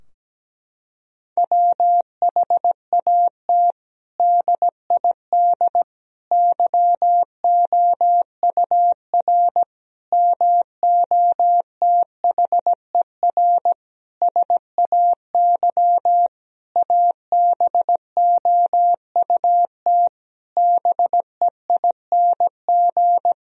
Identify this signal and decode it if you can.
Morse code, heard as a keyed tone, about 17 wpm, 700 Hz. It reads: WHAT DID YOUR MOTHER SAY ABOUT BEING
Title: Sentences from Top 1000 Words - Review Part 3 - 17wpm